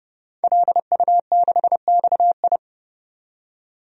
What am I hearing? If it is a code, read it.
Morse code, heard as a keyed tone, about 30 wpm, 700 Hz. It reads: LU6XI
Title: DXCC-100 Call Signs - 5 Characters - 30wpm